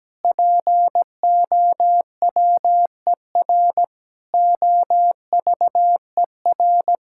Morse code, heard as a keyed tone, about 17 wpm, 700 Hz. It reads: POWER OVER
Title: Sentences from Top 1000 Words - Review Part 6 - 17wpm